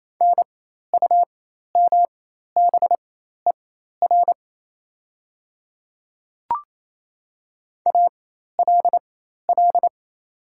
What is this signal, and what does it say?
Morse code, heard as a keyed tone, about 28 wpm, spaced as for 13 wpm, 700 Hz. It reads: NUMBER ALL
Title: 4x Character Spacing - Top 100 Words - 28wpm